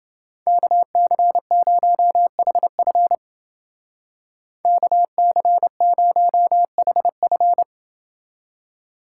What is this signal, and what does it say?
Morse code, heard as a keyed tone, about 28 wpm, 700 Hz. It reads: KC0HF KC0HF